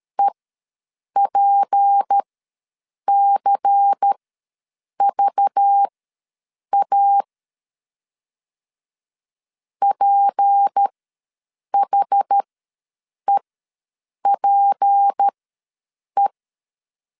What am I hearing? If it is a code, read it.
Morse code, heard as a keyed tone, about 13 wpm, spaced as for 7 wpm, 800 Hz. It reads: EPCVA PHEPE
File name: LEZIONI / 6 PKX0 / 6 P ws 12wpm 3117